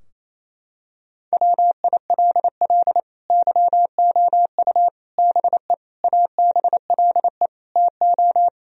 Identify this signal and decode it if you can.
Morse code, heard as a keyed tone, about 28 wpm, 700 Hz. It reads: WILL YOU BE ABLE TO
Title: Sentences from Top 1000 Words - Review Part 2 - 28wpm